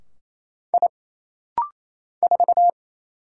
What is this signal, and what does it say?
Morse code, heard as a keyed tone, about 27 wpm, 700 Hz. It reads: I 4
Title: Single Letter-Number - Rapid-Fire 28wpm